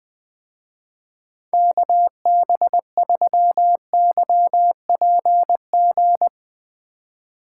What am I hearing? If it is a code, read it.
Morse code, heard as a keyed tone, about 19 wpm, 700 Hz. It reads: KB3YPG
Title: Speed Racing - USA Call Signs 17wpm